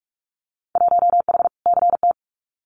Morse code, heard as a keyed tone, about 46 wpm, 700 Hz. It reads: 1H CT
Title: Winter Field Day Exchanges 45wpm